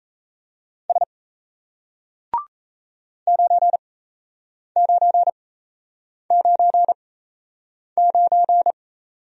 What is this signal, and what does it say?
Morse code, heard as a keyed tone, about 36 wpm, spaced as for 8 wpm, 700 Hz. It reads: S 9999